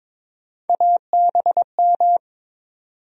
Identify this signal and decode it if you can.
Morse code, heard as a keyed tone, about 21 wpm, 700 Hz. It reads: ABM
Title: USA Call Sign Postfixes - 3 Letters 22wpm